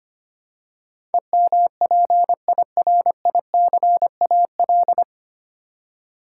Morse code, heard as a keyed tone, about 25 wpm, 700 Hz. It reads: EMPIRICAL
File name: Nine Letter Words - 25wpm